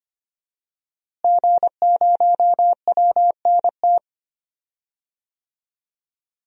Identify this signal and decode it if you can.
Morse code, heard as a keyed tone, about 25 wpm, 700 Hz. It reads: G0WNT
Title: DXCC-100 Call Signs - 5 Characters - Repeated 2x - 25wpm